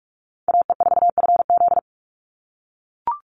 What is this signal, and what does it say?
Morse code, heard as a keyed tone, about 44 wpm, 700 Hz. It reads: AE4FZ
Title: NA SOTA Chaser Callsigns 45wpm